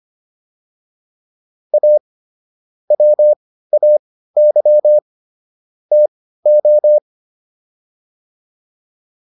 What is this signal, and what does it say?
Morse code, heard as a keyed tone, about 25 wpm, spaced as for 15 wpm, 600 Hz. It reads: A WAY TO